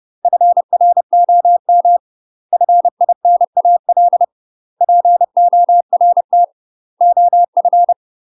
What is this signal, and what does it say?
Morse code, heard as a keyed tone, about 30 wpm, 700 Hz. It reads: FROM FINAL PORT OF
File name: UK MASTER QSOs TRAINING / 30WPM / UK MASTER QSOs 0022